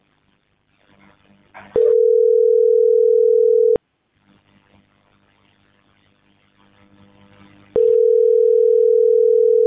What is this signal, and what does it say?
Ringback tone (North America), 440 + 480 Hz.